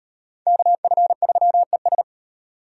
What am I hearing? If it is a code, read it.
Morse code, heard as a keyed tone, about 35 wpm, 700 Hz. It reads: KF3ES